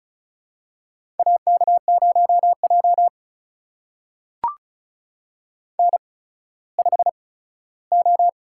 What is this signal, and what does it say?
Morse code, heard as a keyed tone, about 36 wpm, spaced as for 17 wpm, 700 Hz. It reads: AK0J N5O
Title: USA Call Signs - 4 Characters - 8x Character Spacing - 35wpm